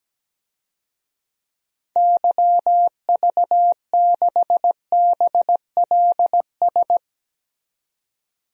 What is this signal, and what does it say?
Morse code, heard as a keyed tone, about 17 wpm, 700 Hz. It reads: YV6BLS
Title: DXCC-100 Call Signs - 6 Characters - 17wpm